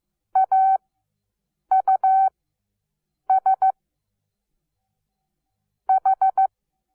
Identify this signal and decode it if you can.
Morse code, heard as a keyed tone, about 15 wpm, spaced as for 7 wpm, 760 Hz. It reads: AUS H
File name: Onlinelehrgang / morsen / DJ4UF Morsen17